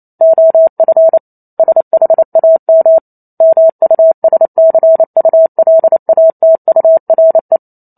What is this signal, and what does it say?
Morse code, heard as a keyed tone, about 28 wpm, 650 Hz. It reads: OF SHAM MUSCULATURE
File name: Chapter 2 - 28 WPM